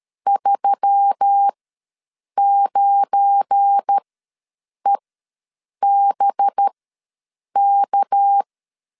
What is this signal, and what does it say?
Morse code, heard as a keyed tone, about 13 wpm, spaced as for 7 wpm, 800 Hz. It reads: 39EBK
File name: LEZIONI / 9 28 ws 12wpm 3354